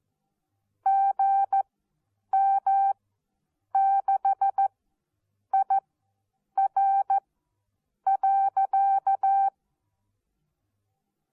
Morse code, heard as a keyed tone, about 14 wpm, spaced as for 8 wpm, 780 Hz. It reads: GM6IR.
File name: Onlinelehrgang / morsen / DJ4UF Morsen14